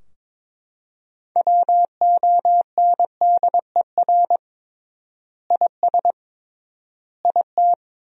Morse code, heard as a keyed tone, about 21 wpm, 700 Hz. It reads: WONDER IS IT
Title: 3x Word-Spacing - Sentences from Top 500 Words - 22wpm